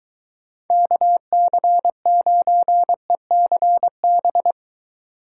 Morse code, heard as a keyed tone, about 23 wpm, 700 Hz. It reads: KC9ECB